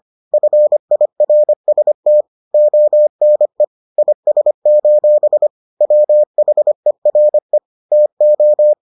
Morse code, heard as a keyed tone, about 25 wpm, 600 Hz. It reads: FIRST ONE IS: WHERE TO